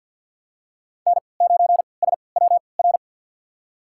Morse code, heard as a keyed tone, about 52 wpm, spaced as for 28 wpm, 700 Hz. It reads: N9SWR
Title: USA Call Signs - 5 Characters - 3x Character Spacing - 50wpm